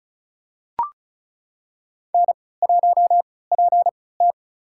Morse code, heard as a keyed tone, about 36 wpm, spaced as for 20 wpm, 700 Hz. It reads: N1PT
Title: USA Call Signs - 4 Characters - 3x Character Spacing - 35wpm